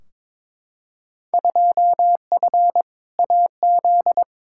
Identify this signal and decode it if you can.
Morse code, heard as a keyed tone, about 22 wpm, 700 Hz. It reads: 2F AZ
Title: Field Day Exchanges - 22wpm